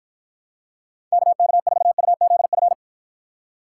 Morse code, heard as a keyed tone, about 52 wpm, 700 Hz. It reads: KK4UZF